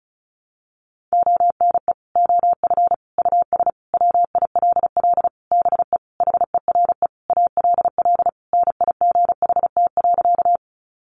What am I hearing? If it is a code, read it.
Morse code, heard as a keyed tone, about 34 wpm, 700 Hz. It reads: ONE OF US WILL BE HERE ALL NIGHT.
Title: Sentences from Top 300 Words 35wpm